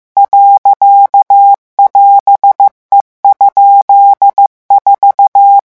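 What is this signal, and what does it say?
Morse code, heard as a keyed tone, about 15 wpm, 800 Hz. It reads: .<AS>E?4
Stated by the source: Morse revision lesson at 14 words per minute.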